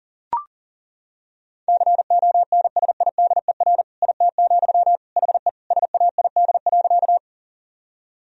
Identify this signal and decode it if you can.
Morse code, heard as a keyed tone, about 37 wpm, 700 Hz. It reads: CONSIDER IT, HE SAID.
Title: Sentences from Top 1000 Words - Review Part 4 - 40wpm